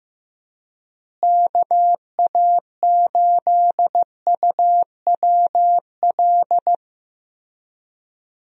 Morse code, heard as a keyed tone, about 15 wpm, 700 Hz. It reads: KA8UWL
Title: USA Call Signs - 6 Characters - Repeated 3x - 15wpm